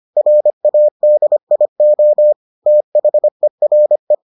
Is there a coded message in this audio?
Yes: Morse code from 0.2 to 4.1 s, heard as a keyed tone, about 25 wpm, 600 Hz. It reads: RADIO THERE